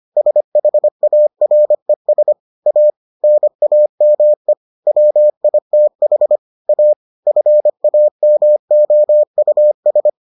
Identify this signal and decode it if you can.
Morse code, heard as a keyed tone, about 25 wpm, 600 Hz. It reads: SHARES A NAME WITH A FAMOUS